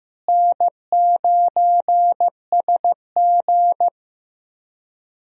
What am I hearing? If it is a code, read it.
Morse code, heard as a keyed tone, about 15 wpm, 700 Hz. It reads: N9SG
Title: USA Call Signs - 4 Characters - Repeated 2x - 15wpm